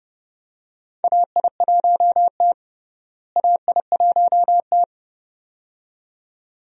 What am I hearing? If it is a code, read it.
Morse code, heard as a keyed tone, about 31 wpm, 700 Hz. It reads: AI1T AI1T
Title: USA Call Signs - 4 Characters - Repeated 2x - 30wpm